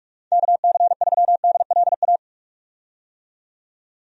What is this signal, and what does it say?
Morse code, heard as a keyed tone, about 46 wpm, 700 Hz. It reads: KC3DLA
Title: USA Call Signs - 6 Characters - 45wpm